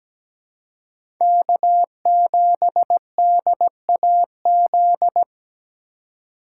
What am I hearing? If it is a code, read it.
Morse code, heard as a keyed tone, about 17 wpm, 700 Hz. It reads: K7DAZ
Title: USA Call Signs - 5 Characters - 2x Character Spacing - 17wpm